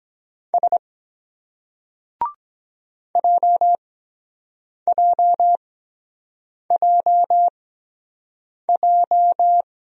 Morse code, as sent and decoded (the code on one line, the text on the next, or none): ... / .--- .--- .--- .---
S JJJJ